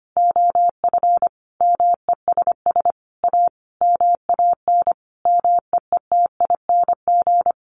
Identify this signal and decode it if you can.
Morse code, heard as a keyed tone, about 25 wpm, 700 Hz. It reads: OF MESS A MAN MEETING
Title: A Sound Of Thunder 25wpm